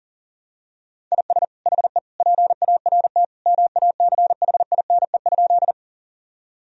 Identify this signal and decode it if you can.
Morse code, heard as a keyed tone, about 40 wpm, 700 Hz. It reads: IS HE PART MACHINE?
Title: Sentences from Top 1000 Words - Review Part 2 - 40wpm